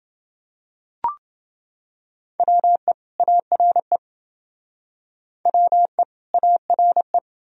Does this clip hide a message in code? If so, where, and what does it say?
Yes: Morse code from 2.4 to 7.2 s, heard as a keyed tone, about 28 wpm, 700 Hz. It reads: WE ARE WE ARE